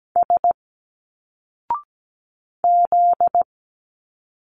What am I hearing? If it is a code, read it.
Morse code, heard as a keyed tone, about 17 wpm, 700 Hz. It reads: S Z
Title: Single Letters 17wpm